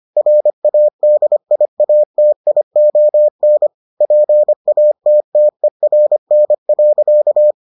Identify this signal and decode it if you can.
Morse code, heard as a keyed tone, about 25 wpm, 600 Hz. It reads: RADIATION PATTERN.